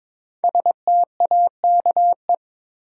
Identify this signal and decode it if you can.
Morse code, heard as a keyed tone, about 22 wpm, 700 Hz. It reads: STAKE
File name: 5 Letter Words - First 3 Letters Given - 22wpm